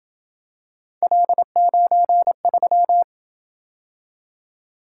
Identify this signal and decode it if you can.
Morse code, heard as a keyed tone, about 27 wpm, 700 Hz. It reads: L93